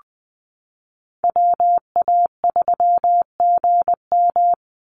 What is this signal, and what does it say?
Morse code, heard as a keyed tone, about 20 wpm, 700 Hz. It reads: WA3GM
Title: Call Signs Repeated 2x - 20wpm